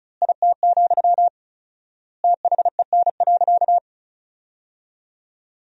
Morse code, heard as a keyed tone, about 36 wpm, 700 Hz. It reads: IT, THEN.
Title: 4x Word-Spacing - Sentences from Top 200 Words - 35wpm